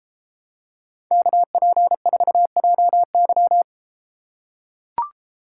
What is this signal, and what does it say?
Morse code, heard as a keyed tone, about 32 wpm, 700 Hz. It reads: KP4JY